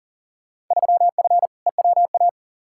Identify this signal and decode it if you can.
Morse code, heard as a keyed tone, about 40 wpm, 700 Hz. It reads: 3F EWA
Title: Field Day Exchanges Encore - 40wpm